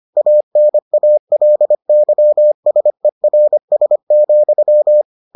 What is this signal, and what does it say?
Morse code, heard as a keyed tone, about 25 wpm, 600 Hz. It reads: ANALYSERS,